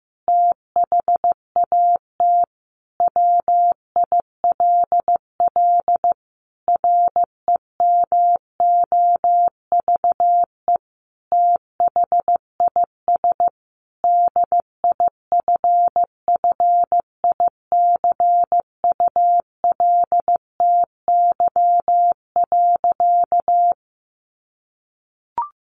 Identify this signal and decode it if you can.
Morse code, heard as a keyed tone, about 15 wpm, 700 Hz. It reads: THAT WILL REMOVE THIS DIFFICULTY.
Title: Sentences from Top 1500 Words 15wpm